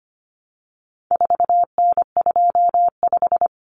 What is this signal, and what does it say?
Morse code, heard as a keyed tone, about 25 wpm, 700 Hz. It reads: 4N25
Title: Common Cut Numbers - 4-Digit 25wpm